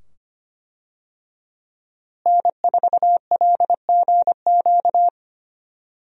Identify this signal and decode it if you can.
Morse code, heard as a keyed tone, about 25 wpm, 700 Hz. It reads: N4LGQ
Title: USA Call Signs - 5 Characters - Repeated 2x - 25wpm